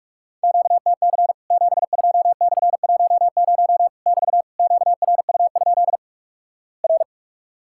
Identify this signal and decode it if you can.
Morse code, heard as a keyed tone, about 46 wpm, 700 Hz. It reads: QTC 72/10 <BT> QRU? R
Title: Worked All Europe DX Contest - QTC Exchange - 45wpm